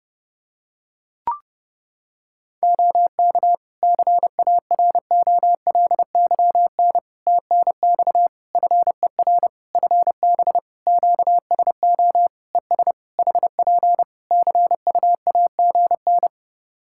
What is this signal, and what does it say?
Morse code, heard as a keyed tone, about 30 wpm, 700 Hz. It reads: OK CAROLYN TNX FER FB QSO ES HP CUAGN
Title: Practice QSO - Exchange 4 - Ending - 30wpm